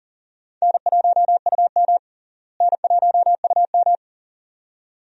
Morse code, heard as a keyed tone, about 38 wpm, 700 Hz. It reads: N1UM N1UM
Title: USA Call Signs - 4 Characters - Repeated 2x - 40wpm